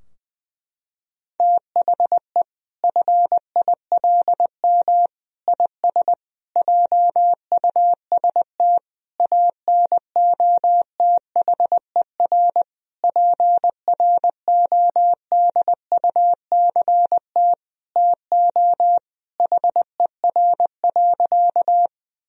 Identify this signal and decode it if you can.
Morse code, heard as a keyed tone, about 20 wpm, 700 Hz. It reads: THE FILM IS JUST ANOTHER PRODUCT TO HER.
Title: Sentences from Top 1000 Words - Review Part 3 - 20wpm